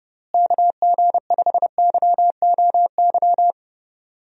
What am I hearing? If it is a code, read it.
Morse code, heard as a keyed tone, about 30 wpm, 700 Hz. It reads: KG5YOY
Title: USA Call Signs - 6 Characters - Repeated 3x - 30wpm